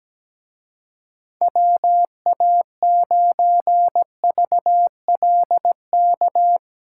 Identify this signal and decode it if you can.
Morse code, heard as a keyed tone, about 18 wpm, 700 Hz. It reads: WA9VLK
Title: Speed Racing - USA 2x3 Call Signs 17wpm